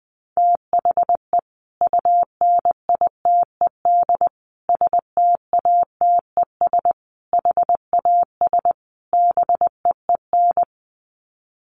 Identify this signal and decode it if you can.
Morse code, heard as a keyed tone, about 20 wpm, 700 Hz. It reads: THE UNITED STATES HAS BEEN
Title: Sets of 5 Words Encore2 20wpm